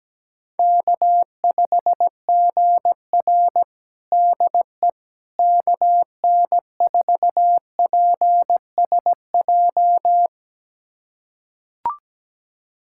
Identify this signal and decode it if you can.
Morse code, heard as a keyed tone, about 17 wpm, 700 Hz. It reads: K5GR DE KN4PSJ
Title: USA Callsign DE Callsign 17wpm